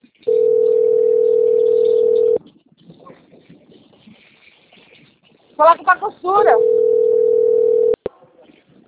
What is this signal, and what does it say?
Ringback tone (North America), 440 + 480 Hz, with speech.
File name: out- - - - - .72049